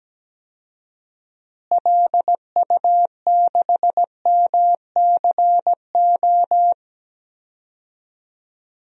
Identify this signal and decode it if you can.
Morse code, heard as a keyed tone, about 17 wpm, 700 Hz. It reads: LU6MCO
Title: DXCC-100 Call Signs - 6 Characters - Repeated 2x - 17wpm